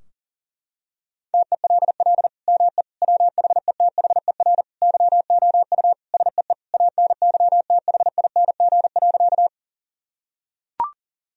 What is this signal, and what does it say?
Morse code, heard as a keyed tone, about 37 wpm, 700 Hz. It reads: TELL ME WHETHER YOU SEE ANYTHING.
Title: Sentences from Top 1000 Words - Review Part 3 - 40wpm